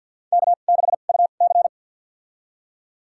Morse code, heard as a keyed tone, about 50 wpm, spaced as for 36 wpm, 700 Hz. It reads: K6UC
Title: USA Call Signs - 4 Characters - 2x Character Spacing - 50wpm